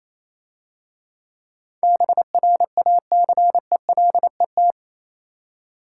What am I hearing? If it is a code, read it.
Morse code, heard as a keyed tone, about 27 wpm, 700 Hz. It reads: BRACELET
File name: Easy Words - Speed-Racing with Hints - 22wpm